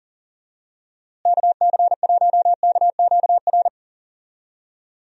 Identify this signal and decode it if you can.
Morse code, heard as a keyed tone, about 40 wpm, 700 Hz. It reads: KC1KQR
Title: USA Call Signs - 6 Characters - Repeated 2x - 40wpm